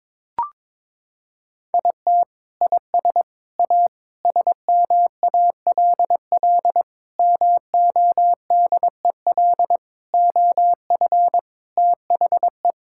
Morse code, heard as a keyed tone, about 22 wpm, 700 Hz. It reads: IT IS A SMALL MODEL OF THE
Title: Sentences from Top 1000 Words - Review Part 5 - 22wpm